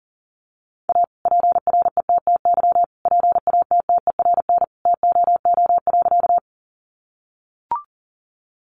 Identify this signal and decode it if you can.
Morse code, heard as a keyed tone, about 40 wpm, 700 Hz. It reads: A PRETTY PATTERN TOO.
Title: Sentences from Top 1000 Words 40wpm